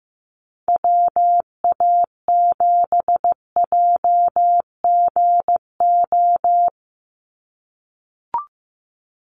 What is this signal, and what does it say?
Morse code, heard as a keyed tone, about 15 wpm, 700 Hz. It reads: WA7JGO
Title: USA Call Signs 15wpm